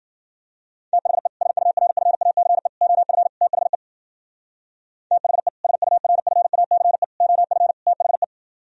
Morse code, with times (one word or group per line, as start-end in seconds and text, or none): 0.9-1.3 THE
1.4-2.7 SURFACE
2.8-3.3 OF
3.4-3.8 THE
5.1-5.5 THE
5.6-7.0 SURFACE
7.2-7.7 OF
7.9-8.2 THE